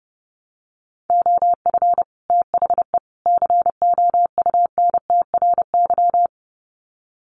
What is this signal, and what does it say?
Morse code, heard as a keyed tone, about 30 wpm, 700 Hz. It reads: OF THE COUNTRY
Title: Sets of 3 Words 30wpm